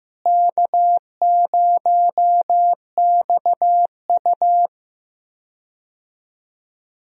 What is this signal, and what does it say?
Morse code, heard as a keyed tone, about 16 wpm, 700 Hz. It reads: K0XU
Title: USA Call Signs - 4 Characters - Repeated 2x - 15wpm